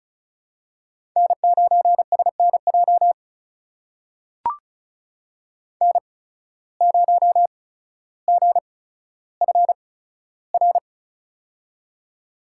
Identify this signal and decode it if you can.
Morse code, heard as a keyed tone, about 36 wpm, spaced as for 15 wpm, 700 Hz. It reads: N9SNJ N0GFR
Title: USA Call Signs - 5 Characters - 8x Character Spacing - 35wpm